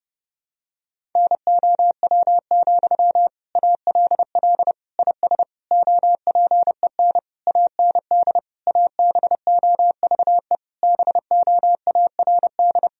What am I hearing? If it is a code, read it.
Morse code, heard as a keyed tone, about 30 wpm, 700 Hz. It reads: NOW, ALL IS OPEN AND ABOVE BOARD
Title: Sentences from Top 1000 Words - Review Part 4 - 30wpm